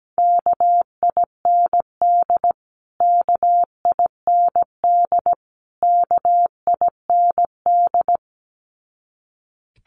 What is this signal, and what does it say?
Morse code, heard as a keyed tone, about 17 wpm, 700 Hz. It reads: KIND KIND KIND
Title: Top 200 Words Repeated 17wpm